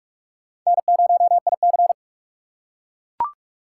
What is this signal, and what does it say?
Morse code, heard as a keyed tone, about 41 wpm, 700 Hz. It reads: N0IC